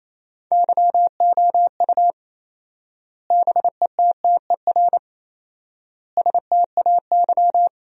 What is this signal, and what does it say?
Morse code, heard as a keyed tone, about 28 wpm, 700 Hz. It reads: YOU BETTER STAY
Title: 4x Word-Spacing - Sentences from Top 500 Words - 28wpm